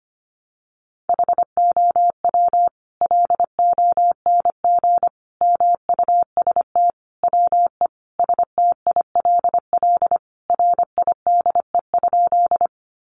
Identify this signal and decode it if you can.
Morse code, heard as a keyed tone, about 25 wpm, 700 Hz. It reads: HOW LONG MUST WE STILL RIDE?
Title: Sentences from Top 1200 Words 25wpm